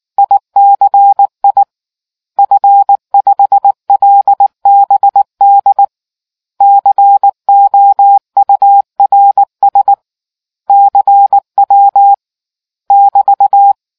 Morse code, heard as a keyed tone, about 19 wpm, 800 Hz. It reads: ICI F5LBD COURS CW <BT>